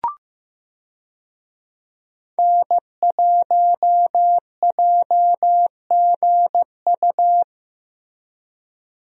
Morse code, heard as a keyed tone, about 16 wpm, 700 Hz. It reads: N1JGU
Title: USA Call Signs - 5 Characters - Repeated 2x - 15wpm